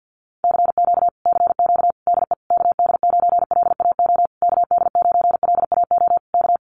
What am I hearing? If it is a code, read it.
Morse code, heard as a keyed tone, about 50 wpm, 700 Hz. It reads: CQ CQ DE KD9LAO KD9LAO K
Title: Common QSO Phrases Encore 50wpm